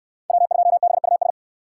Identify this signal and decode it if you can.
Morse code, heard as a keyed tone, about 67 wpm, 700 Hz. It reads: W2BUS